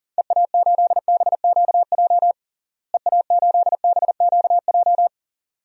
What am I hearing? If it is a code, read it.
Morse code, heard as a keyed tone, about 40 wpm, 700 Hz. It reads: EA8BQJ EA8BQJ
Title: DXCC-100 Call Signs - 6 Characters - Repeated 2x - 40wpm